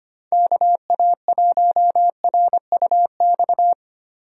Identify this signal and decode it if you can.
Morse code, heard as a keyed tone, about 25 wpm, 700 Hz. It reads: KA1RUX
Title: USA Call Signs - 6 Characters - 25wpm